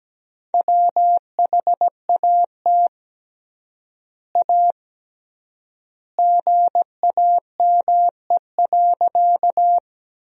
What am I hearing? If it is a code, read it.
Morse code, heard as a keyed tone, about 17 wpm, 700 Hz. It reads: WHAT A GAME.
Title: 3x Word-Spacing - Sentences from Top 500 Words - 17wpm